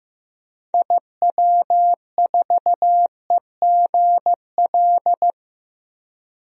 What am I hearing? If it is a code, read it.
Morse code, heard as a keyed tone, about 15 wpm, 700 Hz. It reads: IW4EGL
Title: DXCC-100 Call Signs - 6 Characters - 15wpm